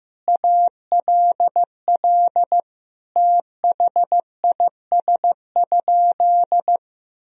Morse code, heard as a keyed tone, about 15 wpm, 700 Hz. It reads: ALL THIS?
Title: Sentences from Top 1000 Words - Review Part 2 - 15wpm